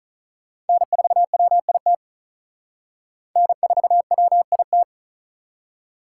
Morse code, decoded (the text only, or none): N4WIT N4WIT